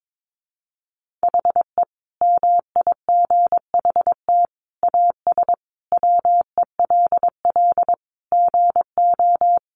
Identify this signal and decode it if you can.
Morse code, heard as a keyed tone, about 22 wpm, 700 Hz. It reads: HE MIGHT AS WELL GO
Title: Sentences from Top 1300 Words 22wpm